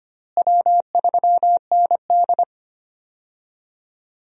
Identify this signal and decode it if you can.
Morse code, heard as a keyed tone, about 25 wpm, 700 Hz. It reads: W3ND
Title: USA Call Signs - 4 Characters - Repeated 3x - 25wpm